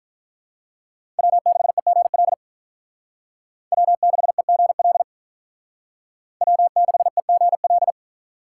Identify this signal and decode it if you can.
Morse code, heard as a keyed tone, about 45 wpm, 700 Hz. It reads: W6EGL W6EGL W6EGL